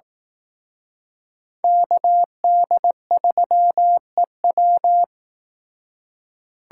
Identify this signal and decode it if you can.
Morse code, heard as a keyed tone, about 19 wpm, 700 Hz. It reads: KD3EW
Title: Speed Racing - USA 2x2 Call Signs 15wpm